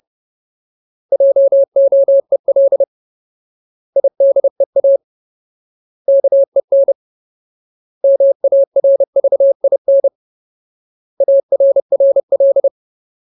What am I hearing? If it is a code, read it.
Morse code, heard as a keyed tone, about 30 wpm, 550 Hz. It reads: JOEL IDEA KEN MARVIN ARRL
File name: Ultimate Word List / 30WPM Ultimate